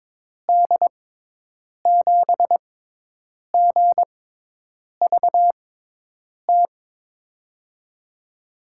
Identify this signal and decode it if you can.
Morse code, heard as a keyed tone, about 23 wpm, spaced as for 8 wpm, 700 Hz. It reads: D7GVT
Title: USA Call Signs - 6x Character Spacing - 22wpm